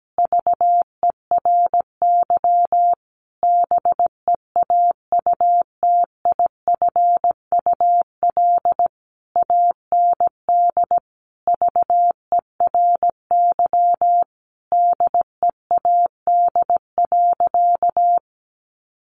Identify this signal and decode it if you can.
Morse code, heard as a keyed tone, about 17 wpm, 700 Hz. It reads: VERY BEAUTIFUL AND VERY DEAD.
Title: A Sound Of Thunder 17wpm